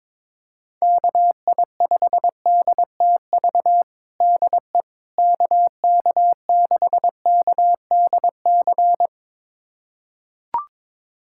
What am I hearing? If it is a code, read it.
Morse code, heard as a keyed tone, about 22 wpm, 700 Hz. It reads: KI5DTV DE KK6KDC
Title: USA Callsign DE Callsign - 6 Characters - 22wpm